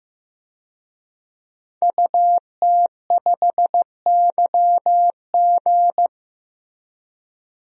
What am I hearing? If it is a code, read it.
Morse code, heard as a keyed tone, about 15 wpm, 700 Hz. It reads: UT5YG
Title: DXCC-100 Call Signs - 5 Characters - Repeated 3x - 15wpm